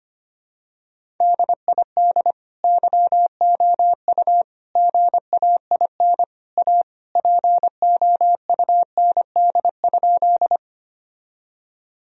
Morse code, heard as a keyed tone, about 25 wpm, 700 Hz. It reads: DID YOU GAIN A POUND?
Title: Sentences from Top 1000 Words - Review Part 2 - 25wpm